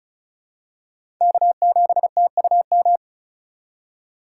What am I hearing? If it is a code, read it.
Morse code, heard as a keyed tone, about 36 wpm, 700 Hz. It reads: K7TUM
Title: USA Call Signs - 5 Characters - 4x Character Spacing - 35wpm